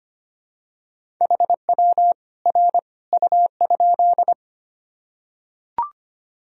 Morse code, heard as a keyed tone, about 25 wpm, 700 Hz. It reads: HW R U?
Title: Sentences from Top 100 Words and Abbreviations 25wpm